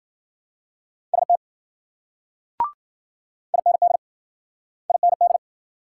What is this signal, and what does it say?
Morse code, heard as a keyed tone, about 50 wpm, 700 Hz. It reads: ST IND IND